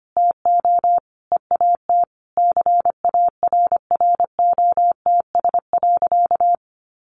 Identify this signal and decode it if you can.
Morse code, heard as a keyed tone, about 25 wpm, 700 Hz. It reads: TO EAT CARROTS.
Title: Easy Phrases 25wpm